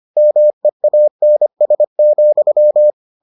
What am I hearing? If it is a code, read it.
Morse code, heard as a keyed tone, about 25 wpm, 600 Hz. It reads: MEANS,